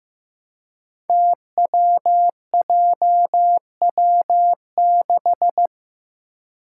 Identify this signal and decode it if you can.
Morse code, heard as a keyed tone, about 15 wpm, 700 Hz. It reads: TWJW6
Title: Five Challenging Letter-Number Combinations 15wpm